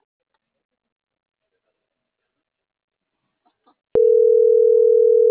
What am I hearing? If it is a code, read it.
Ringback tone (North America), 440 + 480 Hz.